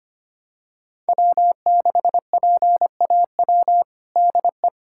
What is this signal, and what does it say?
Morse code, heard as a keyed tone, about 25 wpm, 700 Hz. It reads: W6PAW DE
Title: USA Callsign DE Callsign - 5 Characters - 25wpm